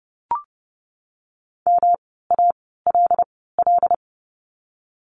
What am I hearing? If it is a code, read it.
Morse code, heard as a keyed tone, about 30 wpm, spaced as for 17 wpm, 700 Hz. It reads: MALL
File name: course / 30:10wpm / 108 Lesson 18f - Letter m - Words 30wpm